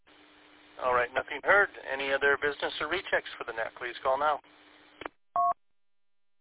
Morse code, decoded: none

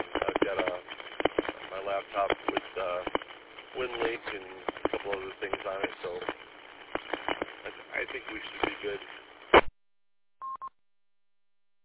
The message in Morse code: none

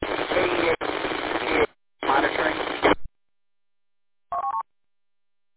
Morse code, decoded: none

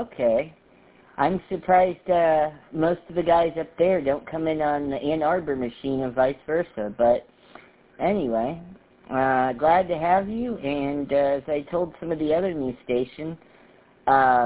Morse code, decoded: none

